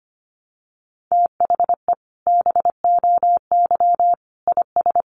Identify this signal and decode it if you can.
Morse code, heard as a keyed tone, about 25 wpm, 700 Hz. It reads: THE BOY IS
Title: Additional Easy Phrases 25wpm